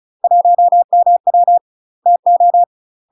Morse code, heard as a keyed tone, about 36 wpm, 700 Hz. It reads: 1MW TO
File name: UK MASTER QSOs TRAINING / 35WPM / UK MASTER QSOs 0000